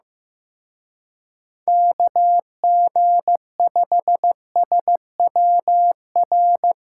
Morse code, heard as a keyed tone, about 18 wpm, 700 Hz. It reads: KG5SWR
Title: Speed Racing - USA 2x3 Call Signs 15wpm